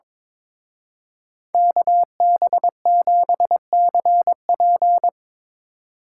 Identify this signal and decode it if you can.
Morse code, heard as a keyed tone, about 25 wpm, 700 Hz. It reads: KB7CP
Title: Speed Racing - USA 2x2 Call Signs 22wpm